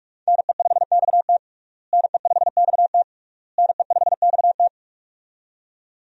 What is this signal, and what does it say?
Morse code, heard as a keyed tone, about 46 wpm, 700 Hz. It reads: NE5XT NE5XT NE5XT
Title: USA Call Signs - 5 Characters - Repeated 3x - 45wpm